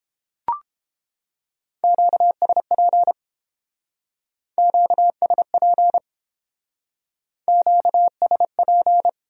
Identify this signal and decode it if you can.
Morse code, heard as a keyed tone, about 29 wpm, 700 Hz. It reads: QSP QSP QSP